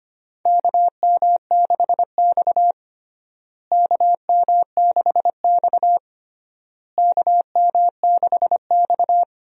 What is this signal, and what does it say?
Morse code, heard as a keyed tone, about 25 wpm, 700 Hz. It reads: KM6X KM6X KM6X
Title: USA Call Signs - 4 Characters - Repeated 3x - 25wpm